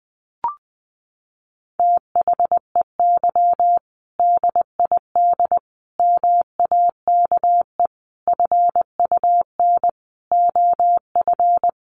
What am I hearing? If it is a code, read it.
Morse code, heard as a keyed tone, about 20 wpm, 700 Hz. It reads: THEY DID MAKE FUN OF
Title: Sentences from Top 1300 Words 20wpm